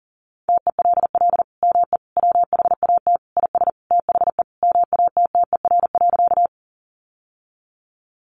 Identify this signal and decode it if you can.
Morse code, heard as a keyed tone, about 40 wpm, 700 Hz. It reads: TELL ME WHAT IS THE MATTER.
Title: Sentences from Top 700 Words 40wpm